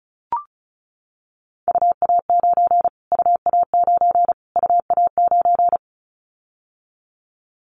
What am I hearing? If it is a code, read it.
Morse code, heard as a keyed tone, about 34 wpm, 700 Hz. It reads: UA9 UA9 UA9
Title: Callsign Prefixes for Top 25 DXCC - Repeated 3x - 35wpm